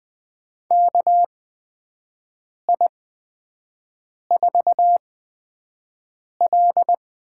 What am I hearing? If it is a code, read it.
Morse code, heard as a keyed tone, about 20 wpm, spaced as for 5 wpm, 700 Hz. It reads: KI4L
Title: USA Call Signs - 8x Character Spacing - 20wpm